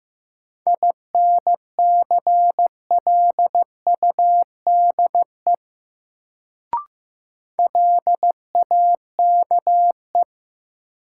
Morse code, heard as a keyed tone, about 15 wpm, 700 Hz. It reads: INCLUDE LAKE
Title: Sentences from Top 1000 Words - Review Part 3 - New Word Drill - 15wpm